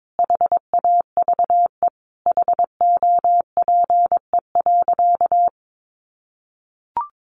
Morse code, heard as a keyed tone, about 22 wpm, 700 Hz. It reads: HAVE HOPE.
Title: Sentences from Top 1000 Words 22wpm